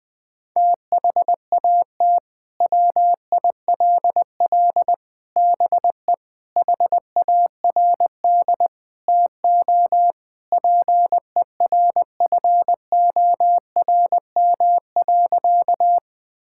Morse code, heard as a keyed tone, about 20 wpm, 700 Hz. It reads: THAT WILL BE HARD TO PERFORM.
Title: Sentences from Top 1000 Words - Review Part 5 - 20wpm